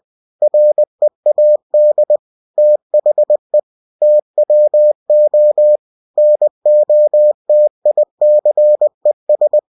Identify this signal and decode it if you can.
Morse code, heard as a keyed tone, about 20 wpm, 600 Hz. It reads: READ THE TWO NOTICES
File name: cw-books / winnie-the-pooh / winnie-20wpm-page-0029